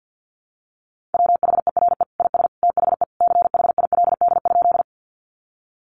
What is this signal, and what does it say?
Morse code, heard as a keyed tone, about 50 wpm, 700 Hz. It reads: WHERE IS THE CHILD?
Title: Sentences from Top 800 Words 50wpm